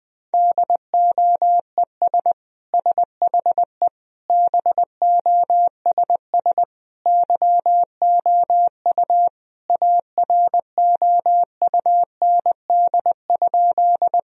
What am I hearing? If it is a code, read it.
Morse code, heard as a keyed tone, about 21 wpm, 700 Hz. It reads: DOES SHE BOSS YOU AROUND?
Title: Sentences from Top 1000 Words - Review Part 4 - 20wpm